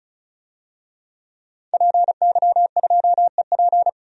Morse code, heard as a keyed tone, about 34 wpm, 700 Hz. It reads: PY2EP
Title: DXCC-100 Call Signs - 5 Characters - 35wpm